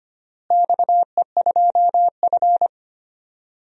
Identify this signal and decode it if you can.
Morse code, heard as a keyed tone, about 25 wpm, 700 Hz. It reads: XE2F
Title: DXCC-100 Call Signs - 4 Characters - Repeated 3x - 25wpm